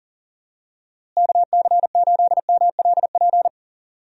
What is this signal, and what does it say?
Morse code, heard as a keyed tone, about 39 wpm, 700 Hz. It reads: KC8MLP
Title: USA Call Signs - 6 Characters - Repeated 3x - 40wpm